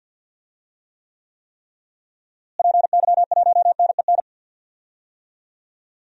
Morse code, heard as a keyed tone, about 53 wpm, 700 Hz. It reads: PY1NEN
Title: DXCC-100 Call Signs - 6 Characters - Repeated 3x - 50wpm